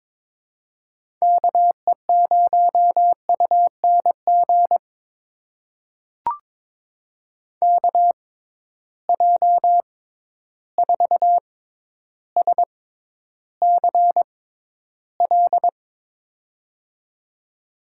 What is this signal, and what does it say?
Morse code, heard as a keyed tone, about 22 wpm, spaced as for 11 wpm, 700 Hz. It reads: KE0UNG KJ4SCL
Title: USA Call Signs - 6 Characters - 6x Character Spacing - 22wpm